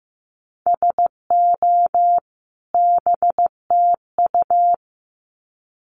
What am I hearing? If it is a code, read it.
Morse code, heard as a keyed tone, about 15 wpm, 700 Hz. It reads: SO BTU
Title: Common QSO Phrases 15wpm